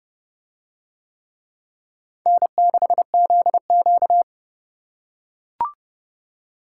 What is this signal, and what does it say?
Morse code, heard as a keyed tone, about 29 wpm, 700 Hz. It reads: N6ZQ